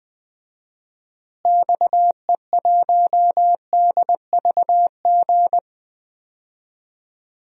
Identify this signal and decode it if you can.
Morse code, heard as a keyed tone, about 20 wpm, 700 Hz. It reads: XE1DVG
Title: DXCC-100 Call Signs - 6 Characters - Repeated 3x - 20wpm